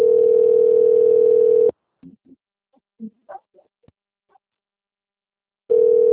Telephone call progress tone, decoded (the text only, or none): ringback tone